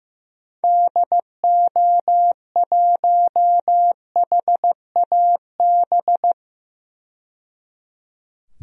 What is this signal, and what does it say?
Morse code, heard as a keyed tone, about 15 wpm, 700 Hz. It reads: DO1HAB
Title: DXCC-100 Call Signs - 6 Characters - 15wpm